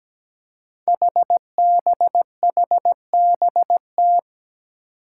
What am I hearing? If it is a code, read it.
Morse code, heard as a keyed tone, about 17 wpm, 700 Hz. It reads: HBHBT
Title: Five Challenging Letter-Number Combinations 17wpm